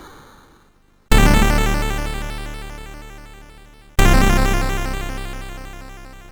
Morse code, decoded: none